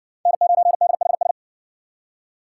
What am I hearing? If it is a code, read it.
Morse code, heard as a keyed tone, about 56 wpm, 700 Hz. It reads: N9DHS